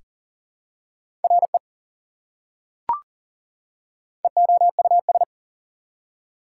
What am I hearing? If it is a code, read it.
Morse code, heard as a keyed tone, about 38 wpm, 700 Hz. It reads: RE EOUS
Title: Prefixes and Suffixes 40wpm